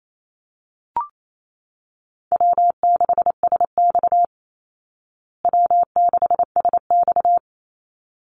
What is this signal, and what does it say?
Morse code, heard as a keyed tone, about 28 wpm, 700 Hz. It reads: W6SX W6SX
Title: Easy Call Signs Repeated 2x - 28wpm